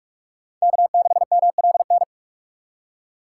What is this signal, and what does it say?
Morse code, heard as a keyed tone, about 44 wpm, 700 Hz. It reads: K6MLN
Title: USA Call Signs - 5 Characters - 4x Character Spacing - 45wpm